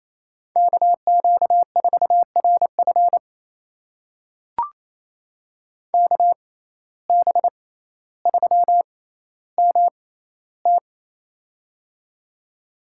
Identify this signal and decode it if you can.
Morse code, heard as a keyed tone, about 28 wpm, spaced as for 14 wpm, 700 Hz. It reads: KQ4RF KB3MT
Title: USA Call Signs - 5 Characters - 6x Character Spacing - 28wpm